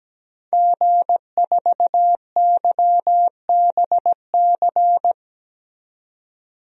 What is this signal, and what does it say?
Morse code, heard as a keyed tone, about 17 wpm, 700 Hz. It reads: G4YBC
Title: DXCC-100 Call Signs - 5 Characters - 17wpm